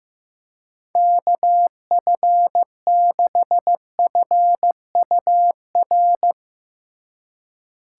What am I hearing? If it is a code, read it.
Morse code, heard as a keyed tone, about 16 wpm, 700 Hz. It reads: KF6FUR
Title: Speed Racing - USA 2x3 Call Signs 15wpm